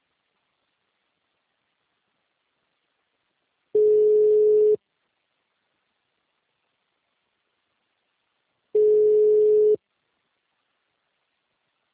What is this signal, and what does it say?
Ringback tone (Europe, 425 Hz).